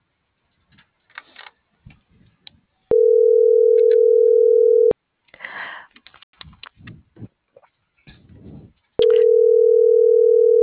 Ringback tone (North America), 440 + 480 Hz.